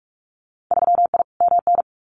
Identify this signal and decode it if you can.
Morse code, heard as a keyed tone, about 46 wpm, 700 Hz. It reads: 3I MN
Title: Winter Field Day Exchanges 45wpm